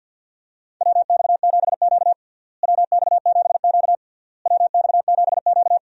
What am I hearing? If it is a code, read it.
Morse code, heard as a keyed tone, about 52 wpm, 700 Hz. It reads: WX7Q WX7Q WX7Q
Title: USA Call Signs - 4 Characters - Repeated 3x - 50wpm